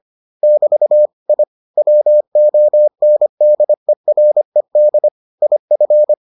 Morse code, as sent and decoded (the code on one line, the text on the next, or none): -...- / .. / .-- --- -. -.. .-. -.. / .. ..-.
<BT> I WONDERED IF